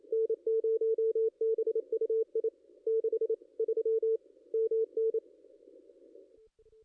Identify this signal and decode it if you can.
Morse code, heard as a keyed tone, about 26 wpm, 450 Hz. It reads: N0BUI 63 MN
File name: contests / ss-cw-k5tr-op-n5rz -2200